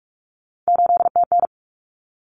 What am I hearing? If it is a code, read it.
Morse code, heard as a keyed tone, about 44 wpm, 700 Hz. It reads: 8TN
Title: Common Cut Numbers - 3-Digit 45wpm